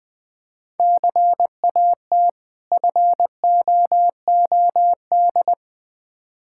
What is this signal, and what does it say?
Morse code, heard as a keyed tone, about 20 wpm, 690 Hz. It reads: CAT FOOD
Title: Difficult Words with Hints - 20wpm